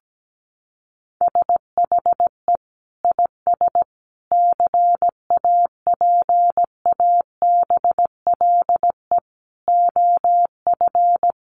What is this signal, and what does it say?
Morse code, heard as a keyed tone, about 17 wpm, 700 Hz. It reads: SHE IS CAPABLE OF
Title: Sentences from Top 1200 Words 17wpm